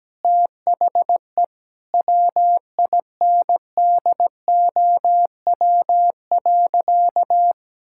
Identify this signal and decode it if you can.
Morse code, heard as a keyed tone, about 17 wpm, 700 Hz. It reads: THE WINDOW.
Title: Sentences from Top 1600 Words 17wpm